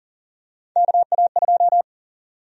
Morse code, heard as a keyed tone, about 40 wpm, 700 Hz. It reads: KA2
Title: USA Call Sign Prefixes - 40wpm